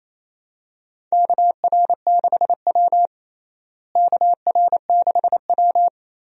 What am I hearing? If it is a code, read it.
Morse code, heard as a keyed tone, about 28 wpm, 700 Hz. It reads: KR6W KR6W
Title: USA Call Signs - 4 Characters - Repeated 2x - 28wpm